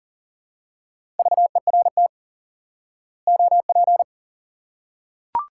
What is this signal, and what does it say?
Morse code, heard as a keyed tone, about 40 wpm, 700 Hz. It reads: VERT OP
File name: IQE Course 3 - Lesson 89 - YAGI - Two Elements - 40wpm